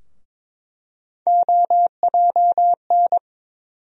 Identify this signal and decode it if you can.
Morse code, heard as a keyed tone, about 22 wpm, 700 Hz. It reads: OJN
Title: USA Call Sign Postfixes - 3 Letters 22wpm